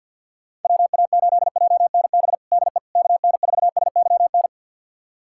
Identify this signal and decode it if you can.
Morse code, heard as a keyed tone, about 53 wpm, 700 Hz. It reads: WA8JNB DE KN4SYN
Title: USA Callsign DE Callsign - 6 Characters - 50wpm